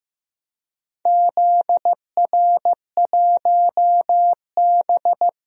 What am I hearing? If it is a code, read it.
Morse code, heard as a keyed tone, about 15 wpm, 700 Hz. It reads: ZR1B
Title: DXCC-100 Call Signs - 4 Characters - Repeated 2x - 15wpm